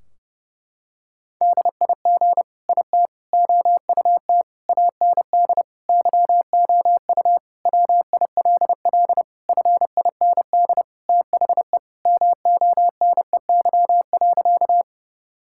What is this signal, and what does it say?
Morse code, heard as a keyed tone, about 32 wpm, 700 Hz. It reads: DIG IT OUT AND YOU WILL FIND THE MONEY.
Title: Sentences from Top 1600 Words 30wpm